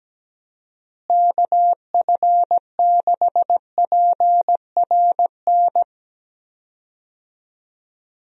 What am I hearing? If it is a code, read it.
Morse code, heard as a keyed tone, about 17 wpm, 700 Hz. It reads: KF6PRN
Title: USA Call Signs - 6 Characters - Repeated 2x - 17wpm